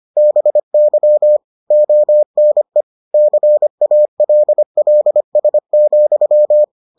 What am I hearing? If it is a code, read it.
Morse code, heard as a keyed tone, about 25 wpm, 600 Hz. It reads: BY ONE CALLS,